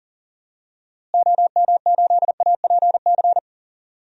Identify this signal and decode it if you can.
Morse code, heard as a keyed tone, about 40 wpm, 700 Hz. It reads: OM8APC
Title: DXCC-100 Call Signs - 6 Characters - Repeated 3x - 40wpm